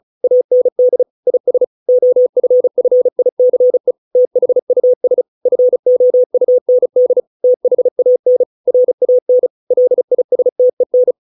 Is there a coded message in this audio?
Yes: Morse code from 0.2 to 11.1 s, heard as a keyed tone, about 34 wpm, 500 Hz. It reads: AND IS OFFICE THUS FOUND THAN RAN LISTEN